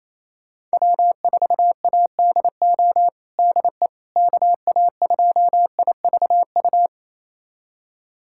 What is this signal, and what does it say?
Morse code, heard as a keyed tone, about 28 wpm, 700 Hz. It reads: W4ADO DE KA2IVU
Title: USA Callsign DE Callsign 28wpm